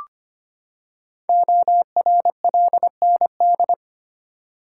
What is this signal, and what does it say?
Morse code, heard as a keyed tone, about 25 wpm, 700 Hz. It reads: ORLND
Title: Code Groups - 5 Letters - 25wpm